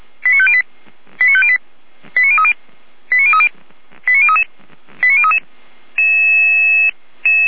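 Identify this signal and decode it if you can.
Five-tone selective call (ZVEI), heard as quick runs of five tones, tones of about 70 ms. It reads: 89508 89508 89030 89030 89033 89033
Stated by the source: Five-tone ZVEI selective call received by SDR, from audio.f85360.feuerwehr-orlamuende.org (Feuerwehr Orlamünde fire brigade).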